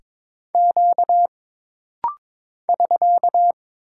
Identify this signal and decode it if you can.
Morse code, heard as a keyed tone, about 22 wpm, 700 Hz. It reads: Q <SK>
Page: This rapid-fire timing is geared to push your ICR (Instant Character Recognition) skills.